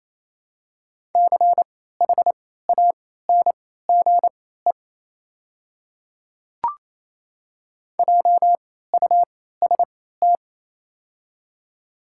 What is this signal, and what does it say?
Morse code, heard as a keyed tone, about 28 wpm, spaced as for 16 wpm, 700 Hz. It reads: CHANGE JUST
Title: 3x Character Spacing - Top 200 Words - 28wpm